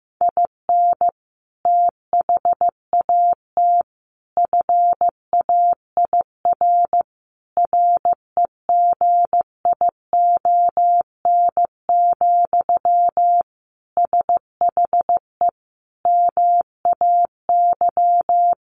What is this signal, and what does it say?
Morse code, heard as a keyed tone, about 14 wpm, 700 Hz. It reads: IN THAT FAIR REGION, SHE MAY
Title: Sentences from Top 1500 Words 15wpm